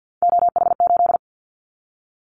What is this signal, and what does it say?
Morse code, heard as a keyed tone, about 50 wpm, 700 Hz. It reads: OH8
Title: Callsign Prefixes for Top 50 DXCC - 50wpm